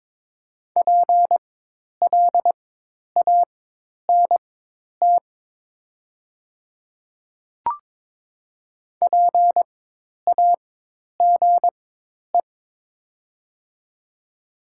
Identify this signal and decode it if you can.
Morse code, heard as a keyed tone, about 21 wpm, spaced as for 10 wpm, 700 Hz. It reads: PLANT PAGE
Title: 4x Character Spacing - Top 300 Words - 22wpm